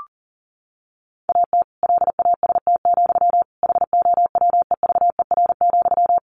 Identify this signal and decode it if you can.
Morse code, heard as a keyed tone, about 40 wpm, 700 Hz. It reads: AT LAST, HOWEVER,
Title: Sentences from Top 1500 Words 40wpm